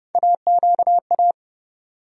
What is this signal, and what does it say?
Morse code, heard as a keyed tone, about 30 wpm, 700 Hz. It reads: AQA
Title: USA Call Sign Postfixes - 3 Letters 30wpm